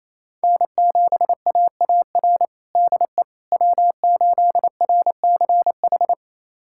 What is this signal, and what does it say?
Morse code, heard as a keyed tone, about 28 wpm, 700 Hz. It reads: N7AAR DE W8RCH
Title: USA Callsign DE Callsign - 5 Characters - 28wpm